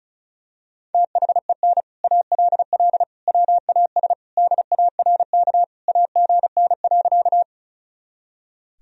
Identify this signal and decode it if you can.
Morse code, heard as a keyed tone, about 34 wpm, 700 Hz. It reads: THEN ALL WAS DARK AGN.
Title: Sentences from Top 500 Words and Abbreviations 35wpm